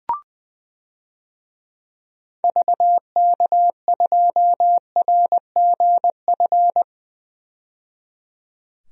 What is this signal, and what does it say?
Morse code, heard as a keyed tone, about 20 wpm, 700 Hz. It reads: VK2RGF
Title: DXCC-100 Call Signs - 6 Characters - 20wpm